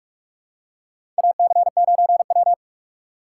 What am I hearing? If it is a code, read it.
Morse code, heard as a keyed tone, about 41 wpm, 700 Hz. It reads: AC9W
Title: USA Call Signs - 4 Characters - 2x Character Spacing - 45wpm